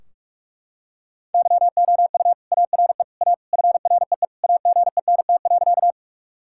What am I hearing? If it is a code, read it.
Morse code, heard as a keyed tone, about 41 wpm, 700 Hz. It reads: YOU ARE A FREE AGENT.
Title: Sentences from Top 1600 Words 45wpm